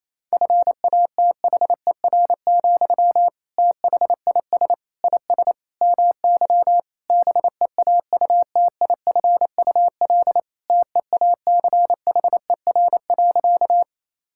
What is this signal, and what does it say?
Morse code, heard as a keyed tone, about 28 wpm, 700 Hz. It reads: FATHER, THIS IS MY BEAUTIFUL TEACHER.
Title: Sentences from Top 1000 Words - Review Part 6 - 28wpm